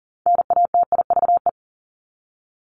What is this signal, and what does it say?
Morse code, heard as a keyed tone, about 40 wpm, 700 Hz. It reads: NATIVE
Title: Top 1751-2000 Words 40wpm